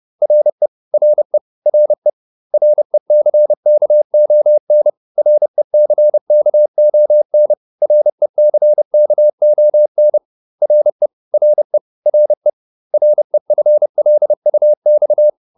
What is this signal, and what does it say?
Morse code, heard as a keyed tone, about 30 wpm, 600 Hz. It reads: RE RE RE RECKON RECKON RECKON RE RE RE REFLUX